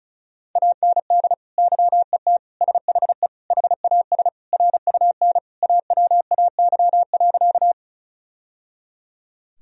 Morse code, heard as a keyed tone, about 36 wpm, 700 Hz. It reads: AND YET SHE HAS RUN AWAY.
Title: Sentences from Top 1000 Words - Review Part 1 - 35wpm